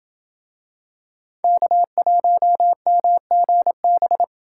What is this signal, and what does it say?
Morse code, heard as a keyed tone, about 27 wpm, 700 Hz. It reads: K1MGB